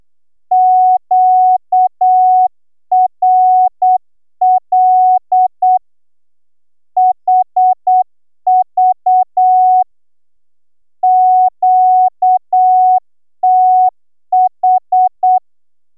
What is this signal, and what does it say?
Morse code, heard as a keyed tone, about 8 wpm, 740 Hz. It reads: QRL HV QTH